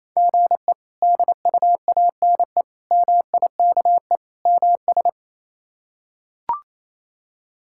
Morse code, heard as a keyed tone, about 28 wpm, 700 Hz. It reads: GE DUANE MIKE MS
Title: K1USN SST QSO Practice - 28wpm